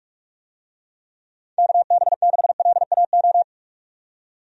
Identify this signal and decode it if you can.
Morse code, heard as a keyed tone, about 46 wpm, 700 Hz. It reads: KB6LAO
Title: USA Call Signs - 6 Characters - 45wpm